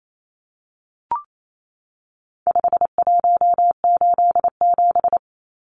Morse code, heard as a keyed tone, about 28 wpm, 700 Hz. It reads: 5187
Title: Numbers - 4-Digit 28wpm